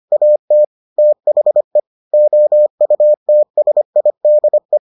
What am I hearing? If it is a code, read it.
Morse code, heard as a keyed tone, about 25 wpm, 600 Hz. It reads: AT THE OUTSIDE